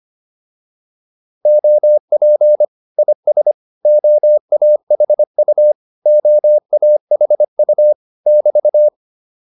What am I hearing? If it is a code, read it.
Morse code, heard as a keyed tone, about 25 wpm, 600 Hz. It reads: OP IS OAHU OAHU <BT>